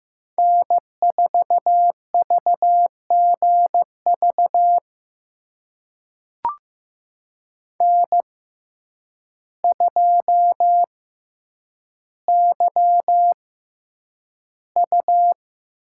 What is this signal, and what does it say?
Morse code, heard as a keyed tone, about 15 wpm, spaced as for 8 wpm, 700 Hz. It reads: N4VGV N2YU
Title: USA Call Signs - 5 Characters - 6x Character Spacing - 15wpm